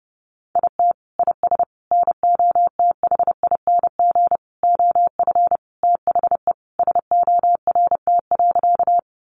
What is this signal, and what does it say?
Morse code, heard as a keyed tone, about 28 wpm, 700 Hz. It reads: IT IS NOTHING OF THE SORT.
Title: Sentences from Top 1300 Words 30wpm